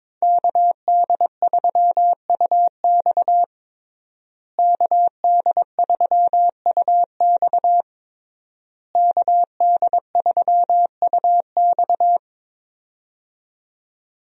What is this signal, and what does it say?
Morse code, heard as a keyed tone, about 22 wpm, 700 Hz. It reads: KD3UX KD3UX KD3UX
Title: USA Call Signs - 5 Characters - Repeated 3x - 22wpm